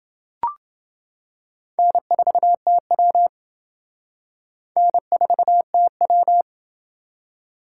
Morse code, heard as a keyed tone, about 28 wpm, 700 Hz. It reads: N4TW N4TW